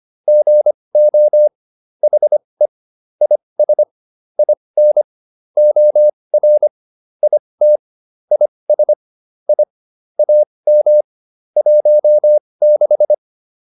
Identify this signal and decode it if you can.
Morse code, heard as a keyed tone, about 25 wpm, spaced as for 20 wpm, 600 Hz. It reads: GO HE IS IN OR IT IS I AM 16
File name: PR102a 20